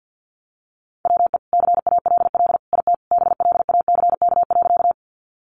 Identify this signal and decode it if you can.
Morse code, heard as a keyed tone, about 50 wpm, 700 Hz. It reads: WE CALL IT BLACK.
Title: Sentences from Top 400 Words 50wpm